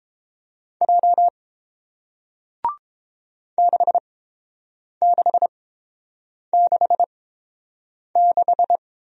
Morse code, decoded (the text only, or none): J 6666